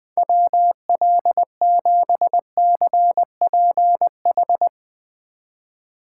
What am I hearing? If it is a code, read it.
Morse code, heard as a keyed tone, about 21 wpm, 700 Hz. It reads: WL7CPH